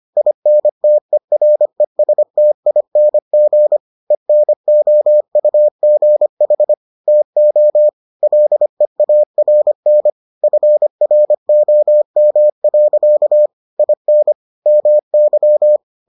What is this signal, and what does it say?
Morse code, heard as a keyed tone, about 25 wpm, 600 Hz. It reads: INTERESTING ENOUGH TO LEARN FROM. IN MY